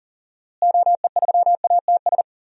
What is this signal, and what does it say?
Morse code, heard as a keyed tone, about 40 wpm, 700 Hz. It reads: OE3ATS
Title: DXCC-100 Call Signs - 6 Characters - Repeated 3x - 40wpm